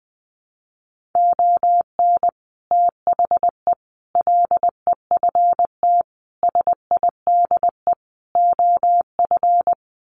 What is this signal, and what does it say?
Morse code, heard as a keyed tone, about 20 wpm, 700 Hz. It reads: ON THE LEFT SIDE OF
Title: Sets of 5 Words Encore2 20wpm